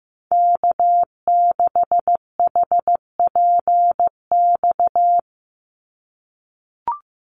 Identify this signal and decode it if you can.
Morse code, heard as a keyed tone, about 15 wpm, 700 Hz. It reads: K6HPX
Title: NA SOTA Activator Callsigns 15wpm